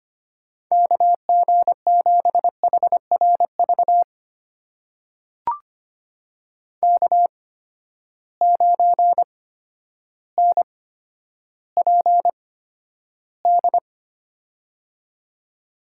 Morse code, heard as a keyed tone, about 25 wpm, spaced as for 11 wpm, 700 Hz. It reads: KG7HRV K9NPD
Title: USA Call Signs - 8x Character Spacing - 25wpm